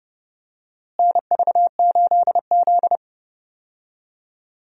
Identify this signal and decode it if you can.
Morse code, heard as a keyed tone, about 30 wpm, 700 Hz. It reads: NV8Z
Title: USA Call Signs - 4 Characters - Repeated 3x - 30wpm